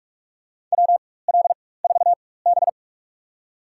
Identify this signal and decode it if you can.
Morse code, heard as a keyed tone, about 44 wpm, spaced as for 21 wpm, 690 Hz. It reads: WL4B
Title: USA Call Signs - 4 Characters - 4x Character Spacing - 45wpm